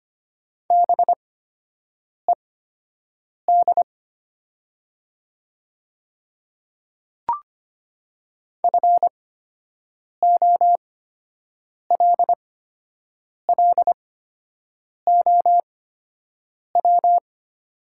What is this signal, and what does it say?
Morse code, heard as a keyed tone, about 25 wpm, spaced as for 7 wpm, 700 Hz. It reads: BED FOLLOW
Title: 8x Character Spacing - Top 500 Words - 25wpm